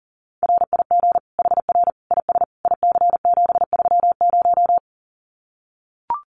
Morse code, heard as a keyed tone, about 40 wpm, 700 Hz. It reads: RIG HR IS IC730
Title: Common QSO Phrases 40wpm